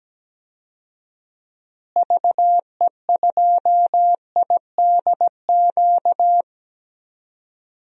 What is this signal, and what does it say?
Morse code, heard as a keyed tone, about 17 wpm, 700 Hz. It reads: VE2IDQ
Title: DXCC-100 Call Signs - 6 Characters - Repeated 2x - 17wpm